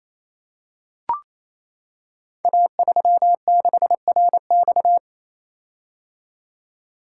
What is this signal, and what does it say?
Morse code, heard as a keyed tone, about 27 wpm, 700 Hz. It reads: A36RX
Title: Code Groups - 5 Letter:Numbers - 28wpm